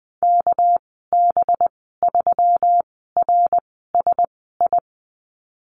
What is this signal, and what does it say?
Morse code, heard as a keyed tone, about 20 wpm, spaced as for 15 wpm, 700 Hz. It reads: KB3RSI
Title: 207 Lesson 33g - Letter j - Call Signs 20wpm